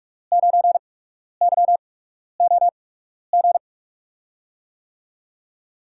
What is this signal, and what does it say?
Morse code, heard as a keyed tone, about 44 wpm, spaced as for 12 wpm, 700 Hz. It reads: 9YOG
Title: USA Call Signs - 6 Characters - 8x Character Spacing - 45wpm